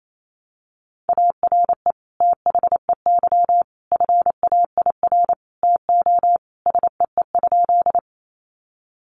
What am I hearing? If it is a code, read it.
Morse code, heard as a keyed tone, about 28 wpm, 700 Hz. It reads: ARE THEY FAIR TO SEE?
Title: Sentences from Top 1300 Words 28wpm